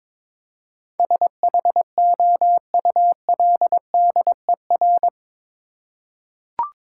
Morse code, heard as a keyed tone, about 22 wpm, 700 Hz. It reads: SHOULDER
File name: practice / 22wpm / Eight Letter Words - 22wpm